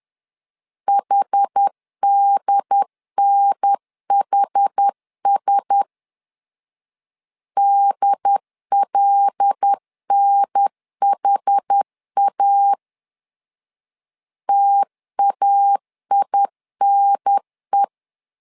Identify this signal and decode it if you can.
Morse code, heard as a keyed tone, about 10 wpm, 800 Hz. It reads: HDNHS DLNHA TAINE
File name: LEZIONI / 5 LBVC / 5 L 10wpm 2259